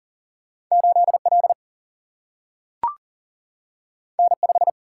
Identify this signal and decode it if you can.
Morse code, heard as a keyed tone, about 40 wpm, 700 Hz. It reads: 8L N5
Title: Single-Letter Single-Number 40wpm